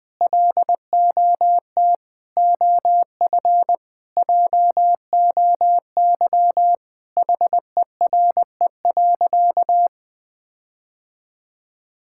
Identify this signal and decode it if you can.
Morse code, heard as a keyed tone, about 21 wpm, 700 Hz. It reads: LOT OF JOY HERE.
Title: Sentences with 5-Word Phrases - 20wpm